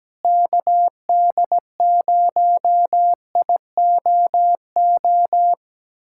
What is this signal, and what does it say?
Morse code, heard as a keyed tone, about 17 wpm, 700 Hz. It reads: KD0IOO
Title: USA Callsign DE Callsign - 6 Characters - 17wpm